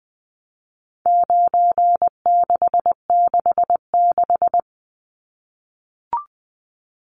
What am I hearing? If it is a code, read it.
Morse code, heard as a keyed tone, about 20 wpm, 700 Hz. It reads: 9666
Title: Numbers - 4-Digit 20wpm